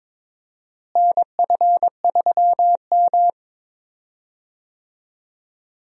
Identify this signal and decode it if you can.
Morse code, heard as a keyed tone, about 22 wpm, 700 Hz. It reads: NF3M
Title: USA Call Signs - 4 Characters - Repeated 2x - 22wpm